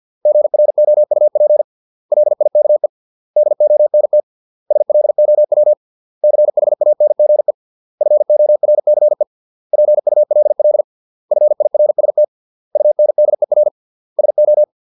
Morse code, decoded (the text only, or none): GROUP LIKE DONT SLOW CHANGE FORCE PULL FIRST UNDER SO